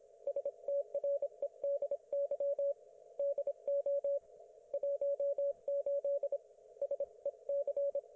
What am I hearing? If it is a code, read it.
Morse code, heard as a keyed tone, about 26 wpm, 560 Hz. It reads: STREDY DO 18 SEC
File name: Archiv / CUC194 / ok1fhi cul194